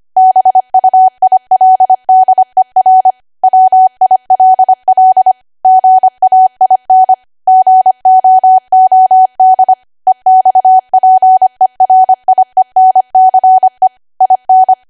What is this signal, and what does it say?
Morse code, heard as a keyed tone, about 25 wpm, 750 Hz. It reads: BUILDER WILL GAIN GOOD EXPERIENCE IN